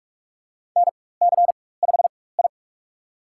Morse code, heard as a keyed tone, about 46 wpm, spaced as for 21 wpm, 700 Hz. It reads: NC5I
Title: USA Call Signs - 4 Characters - 4x Character Spacing - 45wpm